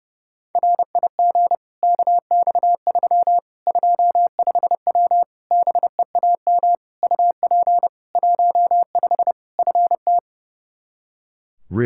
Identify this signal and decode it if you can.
Morse code, heard as a keyed tone, about 30 wpm, 700 Hz. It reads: RIG KX3 25W BEAM UP 15 FT
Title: K1USN SST QSO Practice 30wpm